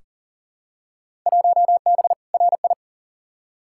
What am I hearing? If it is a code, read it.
Morse code, heard as a keyed tone, about 40 wpm, 700 Hz. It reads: 1B RI
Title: Field Day Exchanges Encore - 40wpm